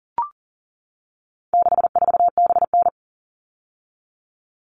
Morse code, heard as a keyed tone, about 38 wpm, 700 Hz. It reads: 64BN
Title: All Cut Numbers - 4-Digit 40wpm